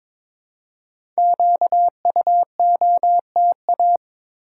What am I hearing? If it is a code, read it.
Morse code, heard as a keyed tone, about 22 wpm, 700 Hz. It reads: QUOTA
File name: 5 Letter Words - First 3 Letters Given - 22wpm